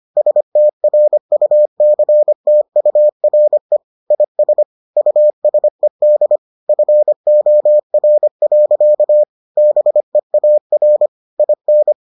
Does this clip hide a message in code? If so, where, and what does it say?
Yes: Morse code from 0.2 to 11.9 s, heard as a keyed tone, about 25 wpm, 600 Hz. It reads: STRUCTURE IS USED FOR. BEAR IN